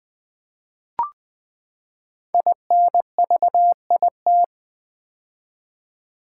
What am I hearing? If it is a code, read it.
Morse code, heard as a keyed tone, about 20 wpm, 700 Hz. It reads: INVIT
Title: Build A Word - 5 Letters At A Time - 10-Letter Words - 20wpm